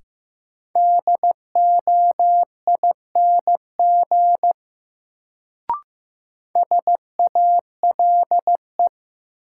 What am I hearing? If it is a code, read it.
Morse code, heard as a keyed tone, about 15 wpm, 700 Hz. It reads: DOING SALE
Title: Sentences from Top 1400 Words - New Word Drill Alt - 15wpm